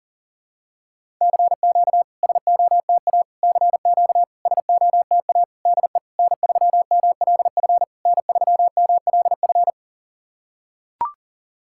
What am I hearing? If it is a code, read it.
Morse code, heard as a keyed tone, about 40 wpm, 700 Hz. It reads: CQ SOTA CQ SOTA DE N3MLF N3MLF
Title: SOTA QSO Practice 40wpm